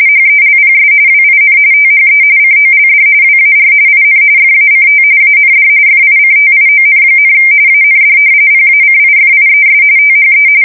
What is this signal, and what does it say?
RTTY, about 45.45 baud. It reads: HELPS WIN THE COLD WAR IN FACT AND FICTION, AND PACIFIES THE
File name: fsk170 / battleshipnj